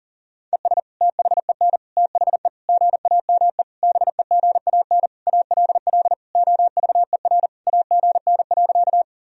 Morse code, heard as a keyed tone, about 40 wpm, 700 Hz. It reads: ES THEN THE GAME BEGAN ALL OVER AGN.
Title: Sentences from Top 500 Words and Abbreviations 40wpm